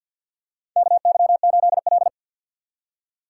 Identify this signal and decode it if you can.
Morse code, heard as a keyed tone, about 53 wpm, 700 Hz. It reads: KY8L
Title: USA Call Signs - 4 Characters - 6x Character Spacing - 50wpm